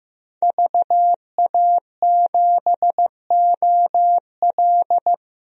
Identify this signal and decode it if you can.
Morse code, heard as a keyed tone, about 15 wpm, 700 Hz. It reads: VA7OL
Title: DXCC-100 Call Signs - 5 Characters - 15wpm